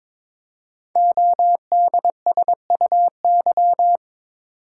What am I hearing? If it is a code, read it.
Morse code, heard as a keyed tone, about 23 wpm, 700 Hz. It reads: ODSUY
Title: Code Groups - 5 Letters - 22wpm